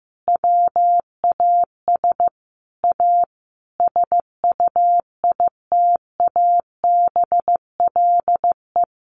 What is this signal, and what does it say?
Morse code, heard as a keyed tone, about 15 wpm, 700 Hz. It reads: WAS A SUITABLE
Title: Sentences from Top 1400 Words 15wpm